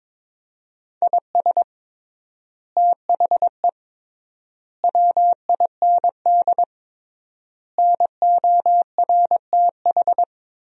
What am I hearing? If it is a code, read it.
Morse code, heard as a keyed tone, about 22 wpm, 700 Hz. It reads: IS THE WIND NORTH
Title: 3x Word-Spacing - Sentences from Top 400 Words - 22wpm